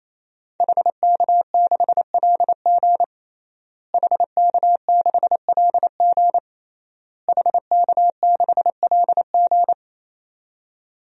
Morse code, heard as a keyed tone, about 28 wpm, 700 Hz. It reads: HK6LG HK6LG HK6LG
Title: DXCC-100 Call Signs - 5 Characters - Repeated 3x - 28wpm